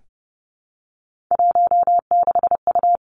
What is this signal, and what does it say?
Morse code, heard as a keyed tone, about 30 wpm, 700 Hz. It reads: 16U
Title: Three Challenging Letter-Number Combinations 30wpm